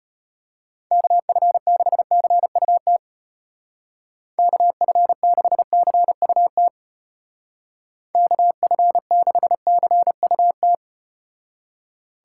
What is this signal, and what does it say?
Morse code, heard as a keyed tone, about 33 wpm, 700 Hz. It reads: KF6CUT KF6CUT KF6CUT